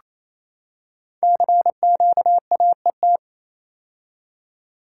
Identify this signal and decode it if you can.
Morse code, heard as a keyed tone, about 27 wpm, 700 Hz. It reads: CQAET
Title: Code Groups - 5 Letter:Numbers - 28wpm